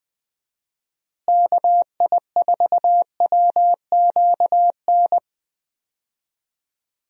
Morse code, heard as a keyed tone, about 20 wpm, 700 Hz. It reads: KI4WQN
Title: USA Call Signs - 6 Characters - Repeated 2x - 20wpm